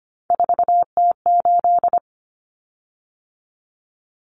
Morse code, heard as a keyed tone, about 25 wpm, 700 Hz. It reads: 4T8
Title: Common Cut Numbers - 3-Digit 25wpm